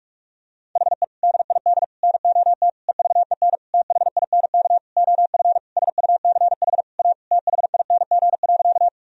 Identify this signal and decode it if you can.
Morse code, heard as a keyed tone, about 46 wpm, 700 Hz. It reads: HE DID NOT EVEN THINK OF SUCH A THING.
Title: Sentences from Top 1000 Words - Review Part 1 - 45wpm